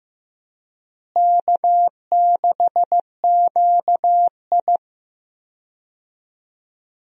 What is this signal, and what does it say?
Morse code, heard as a keyed tone, about 15 wpm, 700 Hz. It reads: K6QI
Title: USA Call Signs - 4 Characters - Repeated 3x - 15wpm